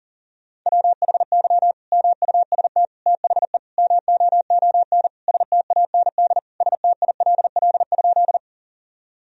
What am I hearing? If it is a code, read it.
Morse code, heard as a keyed tone, about 37 wpm, 700 Hz. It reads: WHY MUST THE MOON STAND STILL?
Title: Sentences from Top 1000 Words - Review Part 3 - 40wpm